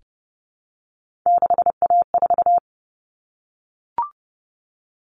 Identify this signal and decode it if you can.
Morse code, heard as a keyed tone, about 30 wpm, 700 Hz. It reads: 6A4
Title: Common Cut Numbers - 3-Digit 30wpm